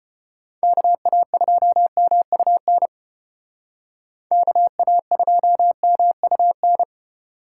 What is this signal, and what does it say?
Morse code, heard as a keyed tone, about 35 wpm, 700 Hz. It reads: KA2MUN KA2MUN